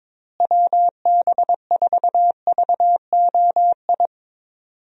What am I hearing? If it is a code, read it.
Morse code, heard as a keyed tone, about 22 wpm, 700 Hz. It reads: WB4VOI
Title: USA Call Signs - 6 Characters - Repeated 3x - 22wpm